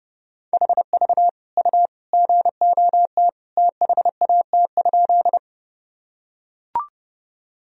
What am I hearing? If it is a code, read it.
Morse code, heard as a keyed tone, about 31 wpm, 700 Hz. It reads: HV U GOT THAT?
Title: Sentences from Top 300 Words and Abbreviations 30wpm